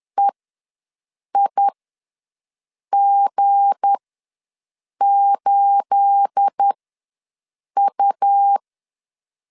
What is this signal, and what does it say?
Morse code, heard as a keyed tone, about 10 wpm, spaced as for 6 wpm, 800 Hz. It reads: EIG8U
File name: LEZIONI / 9 28 ws 10wpm 4124